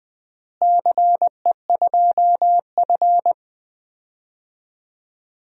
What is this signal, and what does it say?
Morse code, heard as a keyed tone, about 20 wpm, 700 Hz. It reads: CE2F
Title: DXCC-100 Call Signs - 4 Characters - Repeated 3x - 20wpm